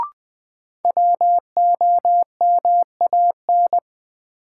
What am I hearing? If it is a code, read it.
Morse code, heard as a keyed tone, about 21 wpm, 700 Hz. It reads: WOMAN
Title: Sentences from Top 1100 Words - New Word Drill Alt - 20wpm